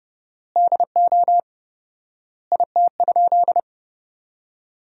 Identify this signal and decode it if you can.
Morse code, heard as a keyed tone, about 32 wpm, 700 Hz. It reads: DO IT?
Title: 4x Word-Spacing - Sentences from Top 100 Words - 30wpm